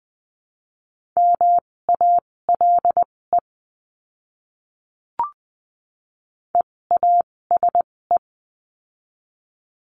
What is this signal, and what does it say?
Morse code, heard as a keyed tone, about 20 wpm, spaced as for 16 wpm, 700 Hz. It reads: MALE EASE
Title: 256 Lesson 42a - Words 12-20wpm